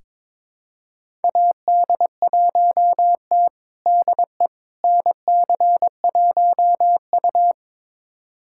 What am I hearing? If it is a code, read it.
Morse code, heard as a keyed tone, about 22 wpm, 700 Hz. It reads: AD1T DE NC1U
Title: USA Callsign DE Callsign - 4 Characters - 22wpm